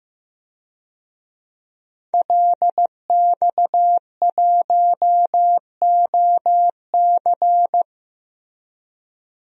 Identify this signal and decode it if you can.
Morse code, heard as a keyed tone, about 14 wpm, 700 Hz. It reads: LX1OC
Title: DXCC-100 Call Signs - 5 Characters - Repeated 2x - 15wpm